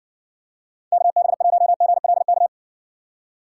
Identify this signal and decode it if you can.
Morse code, heard as a keyed tone, about 60 wpm, 700 Hz. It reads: KB1GLK